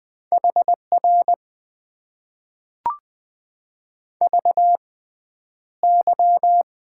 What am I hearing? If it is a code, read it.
Morse code, heard as a keyed tone, about 21 wpm, spaced as for 10 wpm, 700 Hz. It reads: HR VY
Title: Top 90 QSO Elements - 6x Character Spacing - 20wpm